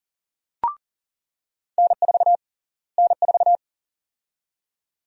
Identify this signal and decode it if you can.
Morse code, heard as a keyed tone, about 40 wpm, 700 Hz. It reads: N4 N4
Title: USA Call Sign Prefixes - Repeated 2x - 40wpm